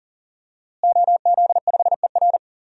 Morse code, heard as a keyed tone, about 40 wpm, 700 Hz. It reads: OZ5ER
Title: DXCC-100 Call Signs - 5 Characters - Repeated 3x - 40wpm